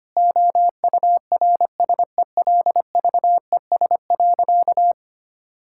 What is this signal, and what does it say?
Morse code, heard as a keyed tone, about 25 wpm, 700 Hz. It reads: OURSELVES.
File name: practice / 25wpm / Difficult Sentences with Hints - 25wpm